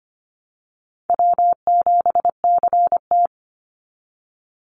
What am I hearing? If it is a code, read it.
Morse code, heard as a keyed tone, about 25 wpm, 700 Hz. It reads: W7CT
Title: Easy Call Signs Repeated 2x - 25wpm